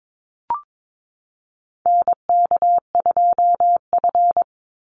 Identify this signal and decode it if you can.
Morse code, heard as a keyed tone, about 22 wpm, 700 Hz. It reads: NK2F
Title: Call Signs Repeated 3x - 22wpm